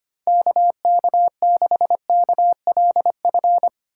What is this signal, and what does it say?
Morse code, heard as a keyed tone, about 25 wpm, 700 Hz. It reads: KK6KLF
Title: USA Call Signs - 6 Characters - 2x Character Spacing - 25wpm